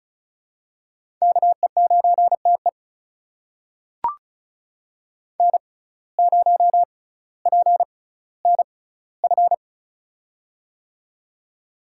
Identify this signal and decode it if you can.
Morse code, heard as a keyed tone, about 36 wpm, spaced as for 18 wpm, 700 Hz. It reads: KE9TE N0PNF
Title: USA Call Signs - 5 Characters - 6x Character Spacing - 35wpm